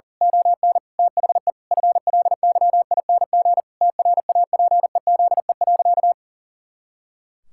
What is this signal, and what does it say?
Morse code, heard as a keyed tone, about 40 wpm, 700 Hz. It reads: ON THE FLYING TRAPEZE.
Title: Pangram Challenge v2 - 40wpm